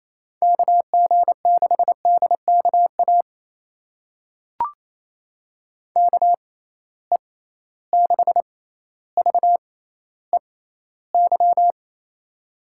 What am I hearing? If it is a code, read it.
Morse code, heard as a keyed tone, about 28 wpm, spaced as for 14 wpm, 700 Hz. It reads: KG6DKA KE6VEY
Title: USA Call Signs - 6 Characters - 6x Character Spacing - 28wpm